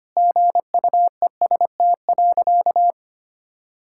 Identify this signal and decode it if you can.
Morse code, heard as a keyed tone, about 25 wpm, 700 Hz. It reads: GUEST.
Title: Sentences from Top 1600 Words 25wpm